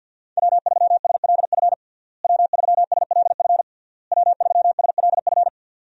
Morse code, heard as a keyed tone, about 52 wpm, 700 Hz. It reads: W3SLF W3SLF W3SLF
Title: USA Call Signs - 5 Characters - Repeated 3x - 50wpm